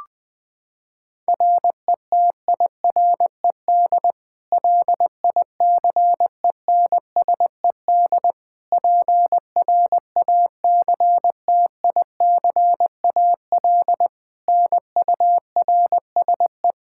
Morse code, heard as a keyed tone, about 21 wpm, 700 Hz. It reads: RETIRED LICENSED PRACTICAL NURSE
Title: Practice QSO - Exchange 3 - More Info - 20wpm